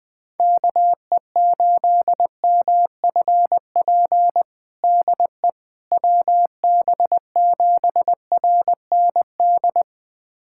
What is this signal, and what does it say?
Morse code, heard as a keyed tone, about 21 wpm, 700 Hz. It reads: KE8MFP DE WB7RND
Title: USA Callsign DE Callsign - 6 Characters - 20wpm